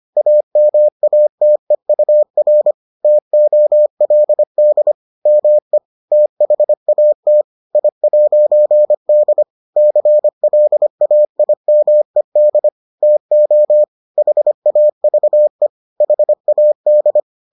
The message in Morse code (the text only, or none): AMATEUR TOLD ME THAT I'D CLAIMED TO HAVE HAD